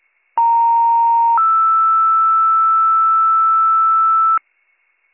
Two-tone page, heard as a one-second tone, then a three-second tone, about 930 then 1320 Hz, with noise. It is heard from 0.4 to 4.4 s.